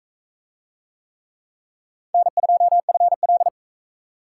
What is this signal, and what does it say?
Morse code, heard as a keyed tone, about 44 wpm, 700 Hz. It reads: N2FL